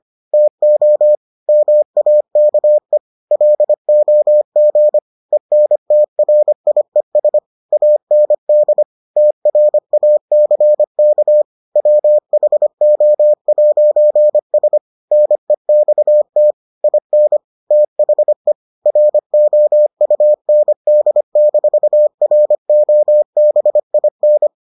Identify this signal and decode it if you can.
Morse code, heard as a keyed tone, about 25 wpm, 600 Hz. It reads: TO MAKE LOG ENTRIES AND TRACK WHO'S NEXT IN THE ROUND-ROBIN